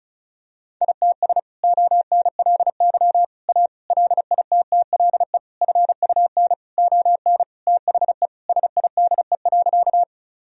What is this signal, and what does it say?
Morse code, heard as a keyed tone, about 34 wpm, 700 Hz. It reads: ITS ONLY A LITTLE FUN ON THE SIDE.
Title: Sentences from Top 1000 Words - Review Part 4 - 35wpm